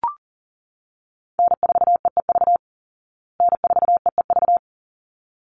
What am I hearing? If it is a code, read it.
Morse code, heard as a keyed tone, about 40 wpm, 700 Hz. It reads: N4EEV N4EEV
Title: Call Signs Repeated 3x - 40wpm